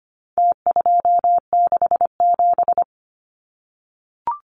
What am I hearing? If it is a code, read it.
Morse code, heard as a keyed tone, about 25 wpm, 700 Hz. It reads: T267
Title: Numbers - Contest Style - 4-Digit 25wpm